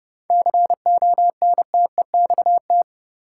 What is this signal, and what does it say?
Morse code, heard as a keyed tone, about 29 wpm, 700 Hz. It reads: CONTEXT
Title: Sentences from Top 1000 Words - Review Part 7 - New Word Drill - 30wpm